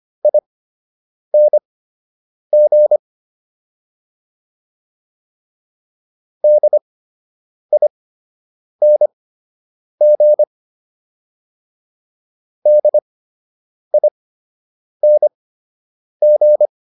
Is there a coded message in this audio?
Yes: Morse code from 0.2 to 16.7 s, heard as a keyed tone, about 25 wpm, spaced as for 8 wpm, 600 Hz. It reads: ING DING DING